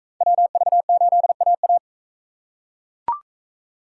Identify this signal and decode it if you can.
Morse code, heard as a keyed tone, about 42 wpm, 700 Hz. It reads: WV8AA